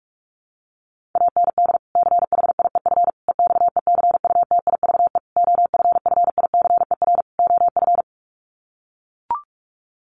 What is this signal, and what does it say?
Morse code, heard as a keyed tone, about 44 wpm, 700 Hz. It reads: AND CHIEF EXECUTIVE OFFICER OF
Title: Sets of 5 Words Encore2 45wpm